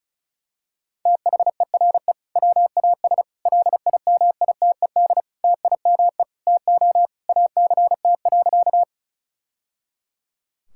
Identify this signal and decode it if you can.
Morse code, heard as a keyed tone, about 36 wpm, 700 Hz. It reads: THERE WAS LIMITED TIME TO ACT.
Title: Sentences from Top 1000 Words - Review Part 6 - 35wpm